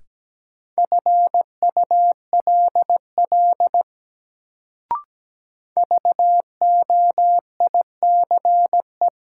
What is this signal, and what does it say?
Morse code, heard as a keyed tone, about 17 wpm, 700 Hz. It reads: FULL VOICE
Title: Sentences from Top 500 Words - New Word Drill - 17wpm